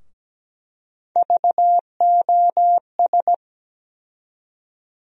Morse code, heard as a keyed tone, about 17 wpm, 700 Hz. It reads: VOS
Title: USA Call Sign Postfixes - 3 Letters 17wpm